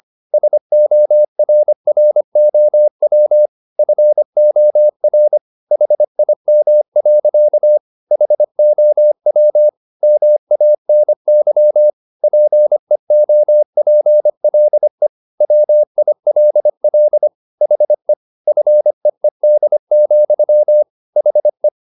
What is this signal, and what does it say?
Morse code, heard as a keyed tone, about 25 wpm, 600 Hz. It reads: SORROW FOR HIM. HOW MANY PEOPLE WILL HE FEED, HE